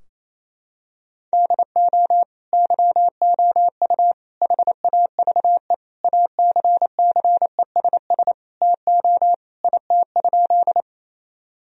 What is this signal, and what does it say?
Morse code, heard as a keyed tone, about 28 wpm, 700 Hz. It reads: DO YOU HAVE ACCESS TO IT?
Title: Sentences from Top 1000 Words - Review Part 4 - 28wpm